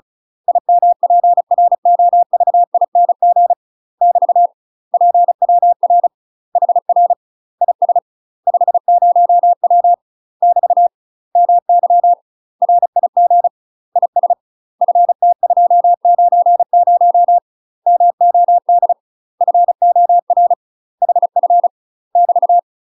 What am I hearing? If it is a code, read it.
Morse code, heard as a keyed tone, about 36 wpm, 700 Hz. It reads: IMPROVING <BT> PWR HR IS 50W <BT> MY RIG IS FT290 MOD FOR HF <BT>
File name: UK MASTER QSOs TRAINING / 35WPM / UK MASTER QSOs 0035